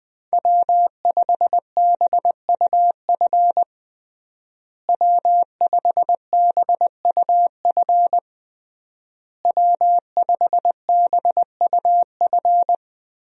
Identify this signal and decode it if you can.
Morse code, heard as a keyed tone, about 20 wpm, 700 Hz. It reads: W5BUF W5BUF W5BUF
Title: USA Call Signs - 5 Characters - Repeated 3x - 20wpm